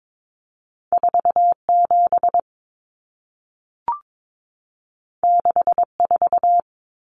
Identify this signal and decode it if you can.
Morse code, heard as a keyed tone, about 22 wpm, 700 Hz. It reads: 47 64
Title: Numbers - 2-Digit 22wpm